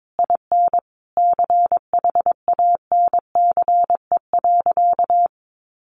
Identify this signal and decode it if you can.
Morse code, heard as a keyed tone, about 22 wpm, 700 Hz. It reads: IN CHANCE.
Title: Sentences from Top 1000 Words 22wpm